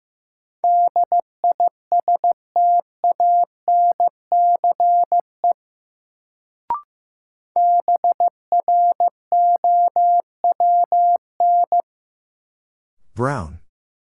Morse code, heard as a keyed tone, about 15 wpm, 700 Hz. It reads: DISTANCE BROWN
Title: Sentences from Top 1200 Words - New Word Drill Alt - 15wpm